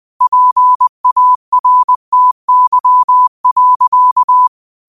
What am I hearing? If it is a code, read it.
Morse code, heard as a keyed tone, about 20 wpm, 1000 Hz. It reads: PARTY.
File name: qotdfiles / 20wpm